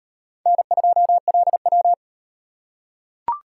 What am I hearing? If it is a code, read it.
Morse code, heard as a keyed tone, about 38 wpm, 700 Hz. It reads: N2LW